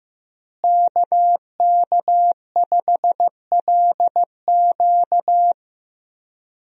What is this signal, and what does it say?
Morse code, heard as a keyed tone, about 15 wpm, 700 Hz. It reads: KK5LQ
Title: USA Call Signs - 5 Characters - Repeated 2x - 15wpm